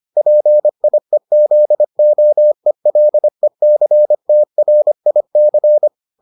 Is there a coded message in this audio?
Yes: Morse code from 0.2 to 5.9 s, heard as a keyed tone, about 25 wpm, 600 Hz. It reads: PIEZOELECTRIC